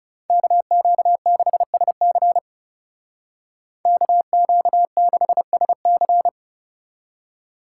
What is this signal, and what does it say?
Morse code, heard as a keyed tone, about 36 wpm, 700 Hz. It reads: KQ6SC KQ6SC
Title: Speed Racing - USA 2x2 Call Signs 30wpm